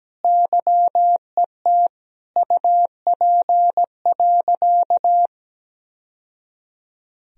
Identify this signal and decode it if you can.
Morse code, heard as a keyed tone, about 17 wpm, 700 Hz. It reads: YET UP.
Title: Sentences from Top 1000 Words - Review Part 3 - 17wpm